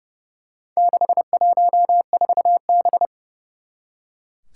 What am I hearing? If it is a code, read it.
Morse code, heard as a keyed tone, about 29 wpm, 700 Hz. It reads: 614B
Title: Four Challenging Letter-Number Combinations 30wpm